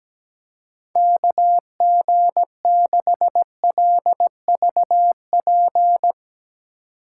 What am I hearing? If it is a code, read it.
Morse code, heard as a keyed tone, about 18 wpm, 700 Hz. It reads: KG6LVP
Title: Speed Racing - USA 2x3 Call Signs 17wpm